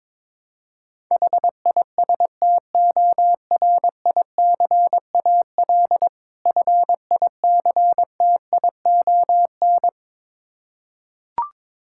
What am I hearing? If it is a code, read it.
Morse code, heard as a keyed tone, about 23 wpm, 690 Hz. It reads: HISTORICAL FICTION
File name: Difficult Words with Hints - 22wpm